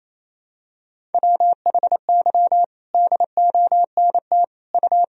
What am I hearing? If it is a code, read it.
Morse code, heard as a keyed tone, about 28 wpm, 700 Hz. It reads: WHY DONT U
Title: Sentences from Top 300 Words and Abbreviations 28wpm